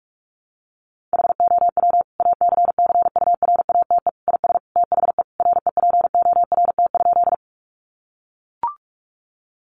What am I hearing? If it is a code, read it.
Morse code, heard as a keyed tone, about 44 wpm, 700 Hz. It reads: HOW ACCURATE IS THE REPORT?
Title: Sentences from Top 1200 Words 45wpm